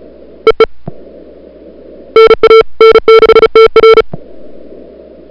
Morse code, heard as a keyed tone, about 32 wpm, 440 Hz. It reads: EE NA N6TR
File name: n6tr / Fab2007CWSprintHour1